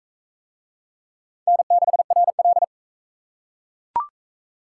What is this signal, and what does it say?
Morse code, heard as a keyed tone, about 40 wpm, 700 Hz. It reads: N6RL